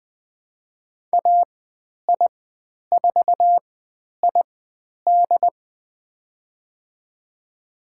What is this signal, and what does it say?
Morse code, heard as a keyed tone, about 20 wpm, spaced as for 10 wpm, 700 Hz. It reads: AI4ID
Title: 095 Lesson 16g - Letter c - Call Signs 20wpm